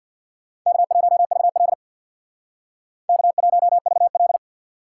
Morse code, heard as a keyed tone, about 62 wpm, 700 Hz. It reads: K1VL K1VL
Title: Speed Racing - USA 1x2 Call Signs 50wpm